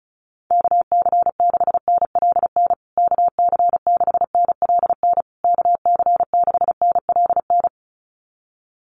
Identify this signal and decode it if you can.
Morse code, heard as a keyed tone, about 34 wpm, 700 Hz. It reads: KC6NLN KC6NLN KC6NLN
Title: USA Call Signs Repeated 3x 35wpm